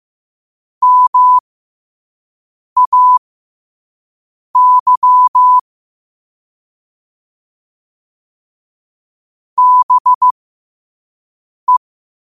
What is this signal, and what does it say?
Morse code, heard as a keyed tone, about 15 wpm, spaced as for 5 wpm, 1000 Hz. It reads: MAY BE
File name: qotdfiles / 5wpm